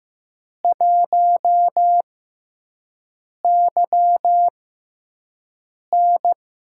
Morse code, heard as a keyed tone, about 15 wpm, spaced as for 5 wpm, 700 Hz. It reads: 1YN
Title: USA Call Signs - 6x Character Spacing - 15wpm